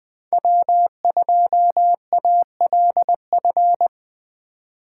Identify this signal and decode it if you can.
Morse code, heard as a keyed tone, about 20 wpm, 700 Hz. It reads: W2ALF
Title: Speed Racing - USA 1x3 Call Signs 17wpm